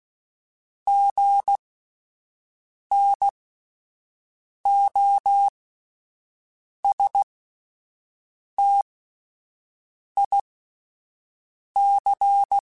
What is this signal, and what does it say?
Morse code, heard as a keyed tone, about 16 wpm, spaced as for 5 wpm, 800 Hz. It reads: GNOSTIC